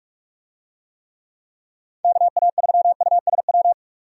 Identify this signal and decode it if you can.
Morse code, heard as a keyed tone, about 46 wpm, 700 Hz. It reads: KA3USW
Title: USA Call Signs - 6 Characters - 45wpm